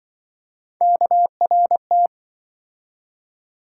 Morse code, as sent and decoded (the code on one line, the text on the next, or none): -.- .-. -
KRT